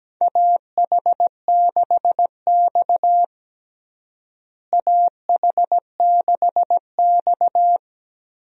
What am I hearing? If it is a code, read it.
Morse code, heard as a keyed tone, about 17 wpm, 700 Hz. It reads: AH6X AH6X
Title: USA Call Signs - 4 Characters - Repeated 2x - 17wpm